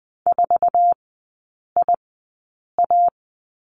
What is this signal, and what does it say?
Morse code, heard as a keyed tone, about 20 wpm, spaced as for 8 wpm, 700 Hz. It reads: 4IA
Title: 047 Lesson 9e - Number 4 - Three Characters 20wpm